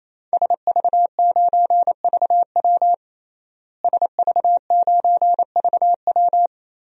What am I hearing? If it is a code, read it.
Morse code, heard as a keyed tone, about 28 wpm, 700 Hz. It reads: SV9VW SV9VW
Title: DXCC-100 Call Signs - 5 Characters - Repeated 3x - 28wpm